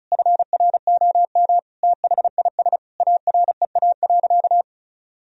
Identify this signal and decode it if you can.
Morse code, heard as a keyed tone, about 34 wpm, 700 Hz. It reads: FROM THIS AREA.
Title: Sentences from Top 1000 Words - Review Part 2 - 35wpm